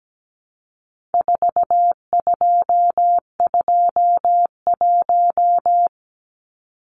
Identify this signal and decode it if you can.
Morse code, heard as a keyed tone, about 17 wpm, 700 Hz. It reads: 4221
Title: Numbers - 4-Digit 17wpm